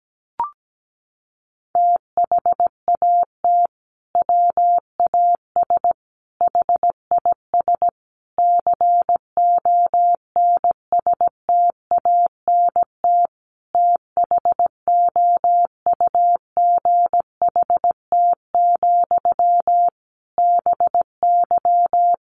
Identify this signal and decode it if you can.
Morse code, heard as a keyed tone, about 17 wpm, 700 Hz. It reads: THAT WAS HIS CONSTANT THOUGHT, BY
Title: Sentences from Top 1500 Words 17wpm